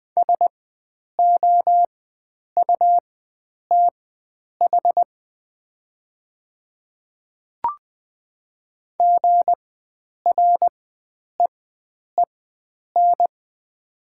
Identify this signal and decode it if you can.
Morse code, heard as a keyed tone, about 20 wpm, spaced as for 9 wpm, 700 Hz. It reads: SOUTH GREEN
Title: 4x Character Spacing - Top 500 Words - 20wpm